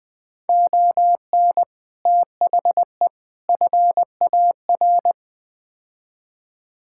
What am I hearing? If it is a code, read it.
Morse code, heard as a keyed tone, about 20 wpm, 700 Hz. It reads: ON THE FAR
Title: Build A Phrase - 5-Word Phrases - 20wpm